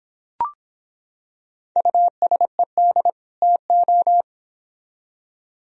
Morse code, heard as a keyed tone, about 25 wpm, 700 Hz. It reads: USED TO